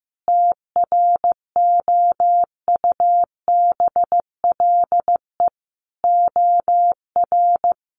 Morse code, heard as a keyed tone, about 15 wpm, 700 Hz. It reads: TROUBLE OR
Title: Sentences from Top 1500 Words 15wpm